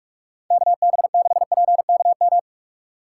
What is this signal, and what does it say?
Morse code, heard as a keyed tone, about 42 wpm, 700 Hz. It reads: KB6PKM
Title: USA Call Signs - 6 Characters - 3x Character Spacing - 45wpm